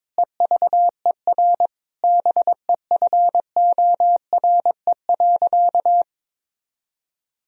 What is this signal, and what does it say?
Morse code, heard as a keyed tone, about 22 wpm, 700 Hz. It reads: EVER BEFORE.
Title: Sentences from Top 1000 Words - Review Part 3 - 22wpm